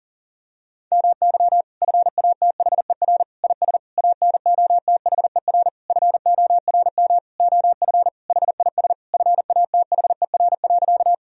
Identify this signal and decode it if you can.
Morse code, heard as a keyed tone, about 40 wpm, 700 Hz. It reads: MY FATHER IS ANOTHER FORM OF HIS FATHER.
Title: Sentences from Top 1000 Words - Review Part 1 - 40wpm